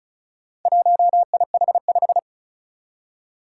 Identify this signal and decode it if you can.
Morse code, heard as a keyed tone, about 36 wpm, 700 Hz. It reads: 1IH5
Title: Four Challenging Letter-Number Combinations 35wpm